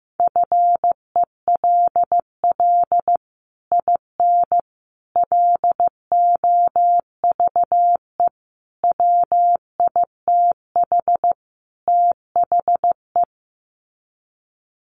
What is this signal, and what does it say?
Morse code, heard as a keyed tone, about 14 wpm, 700 Hz. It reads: FELL IN LOVE WITH THE
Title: Sets of 5 Words Encore2 15wpm